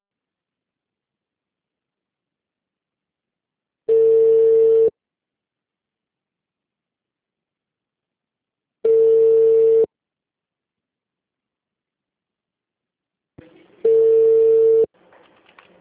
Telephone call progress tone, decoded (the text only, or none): ringback tone